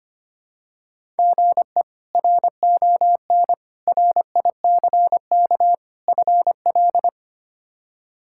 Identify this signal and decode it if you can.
Morse code, heard as a keyed tone, about 25 wpm, 700 Hz. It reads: GE RON RICK FL
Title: K1USN SST QSO Practice - 25wpm